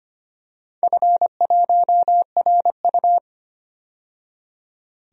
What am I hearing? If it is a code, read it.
Morse code, heard as a keyed tone, about 25 wpm, 700 Hz. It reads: F1RU
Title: DXCC-100 Call Signs - 4 Characters - 25wpm